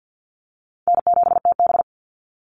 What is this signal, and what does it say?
Morse code, heard as a keyed tone, about 50 wpm, 700 Hz. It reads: N7TB
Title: Call Signs Repeated 2x - 50wpm